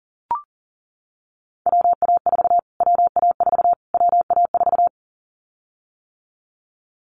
Morse code, heard as a keyed tone, about 40 wpm, 700 Hz. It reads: WA4 WA4 WA4
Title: Callsign Prefixes for Top 25 DXCC - Repeated 3x - 40wpm